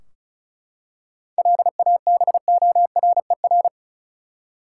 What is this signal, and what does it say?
Morse code, heard as a keyed tone, about 36 wpm, 700 Hz. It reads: LABORER
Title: Practice QSO - Exchange 3 - More Info - 35wpm